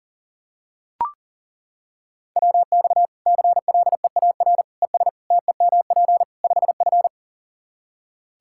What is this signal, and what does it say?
Morse code, heard as a keyed tone, about 40 wpm, 700 Hz. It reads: WX CLEAR ES TEMP 5F
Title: Practice QSO - Exchange 2 - Info - 40wpm